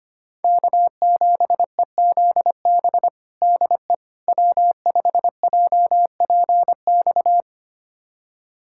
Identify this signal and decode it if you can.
Morse code, heard as a keyed tone, about 25 wpm, 700 Hz. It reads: K7EZB DE W5JPX
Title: USA Callsign DE Callsign - 5 Characters - 25wpm